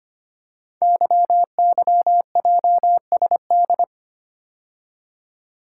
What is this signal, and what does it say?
Morse code, heard as a keyed tone, about 25 wpm, 700 Hz. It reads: YYJSD
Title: Code Groups - 5 Letters - 25wpm